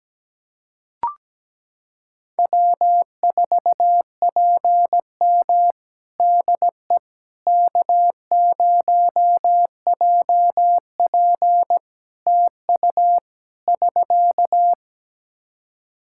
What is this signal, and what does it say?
Morse code, heard as a keyed tone, about 17 wpm, 700 Hz. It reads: W4PM DE K0JP TU <SK>
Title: Practice QSO - Exchange 4 - Ending - 17wpm